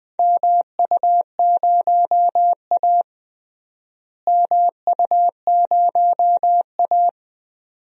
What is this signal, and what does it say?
Morse code, heard as a keyed tone, about 20 wpm, 700 Hz. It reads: MU0A MU0A
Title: DXCC-100 Call Signs - 4 Characters - Repeated 2x - 20wpm